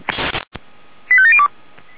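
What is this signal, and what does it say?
Five-tone selective call (ZVEI), heard as quick runs of five tones, tones of about 70 ms. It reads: 85702
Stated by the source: ZVEI five-tone call received with an SDR, from audio.f85360.feuerwehr-orlamuende.org.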